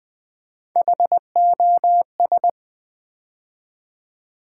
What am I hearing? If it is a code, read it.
Morse code, heard as a keyed tone, about 20 wpm, 700 Hz. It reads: HOS
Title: Build A Word - 3 Letters At A Time - 9-Letter Words - 20wpm